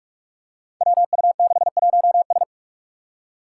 Morse code, heard as a keyed tone, about 44 wpm, 700 Hz. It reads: WU61S
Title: Five Challenging Letter-Number Combinations 45wpm